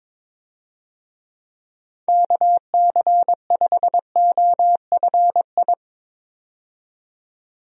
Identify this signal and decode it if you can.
Morse code, heard as a keyed tone, about 21 wpm, 700 Hz. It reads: KC5OFI
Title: USA Call Signs - 6 Characters - 22wpm